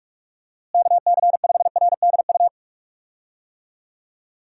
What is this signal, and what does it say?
Morse code, heard as a keyed tone, about 46 wpm, 700 Hz. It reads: KC5RDU
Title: USA Call Signs - 6 Characters - Repeated 3x - 45wpm